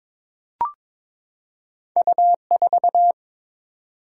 Morse code, heard as a keyed tone, about 22 wpm, 700 Hz. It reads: U4
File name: practice / 22wpm / Two Letter-Number Combinations - 22wpm